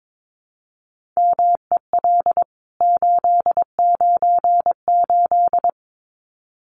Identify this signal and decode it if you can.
Morse code, heard as a keyed tone, about 22 wpm, 700 Hz. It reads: MEL 898
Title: CWT Exchange 22wpm